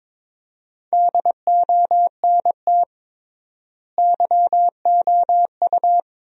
Morse code, heard as a keyed tone, about 22 wpm, 700 Hz. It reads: DONT YOU
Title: 3x Word-Spacing - Sentences from Top 300 Words - 22wpm